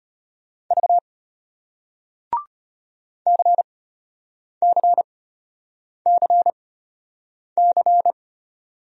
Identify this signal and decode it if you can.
Morse code, heard as a keyed tone, about 33 wpm, spaced as for 8 wpm, 700 Hz. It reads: V CCCC